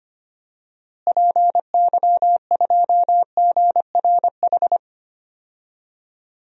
Morse code, heard as a keyed tone, about 25 wpm, 700 Hz. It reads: PY2GRH
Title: DXCC-100 Call Signs - 6 Characters - Repeated 2x - 25wpm